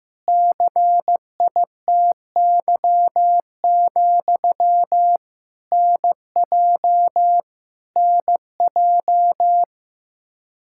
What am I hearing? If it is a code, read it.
Morse code, heard as a keyed tone, about 15 wpm, 700 Hz. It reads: CITY, NJ NJ
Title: Practice QSO - Exchange 1 - Beginning - 15wpm